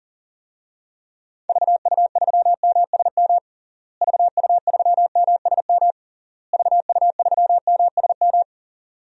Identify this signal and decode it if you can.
Morse code, heard as a keyed tone, about 40 wpm, 700 Hz. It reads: VU3MSM VU3MSM VU3MSM
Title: DXCC-100 Call Signs - 6 Characters - Repeated 3x - 40wpm